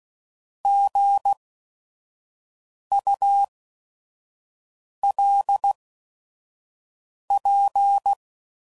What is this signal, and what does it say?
Morse code, heard as a keyed tone, about 16 wpm, spaced as for 5 wpm, 800 Hz. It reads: GULP
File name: Prefixes / By First Letter / Gs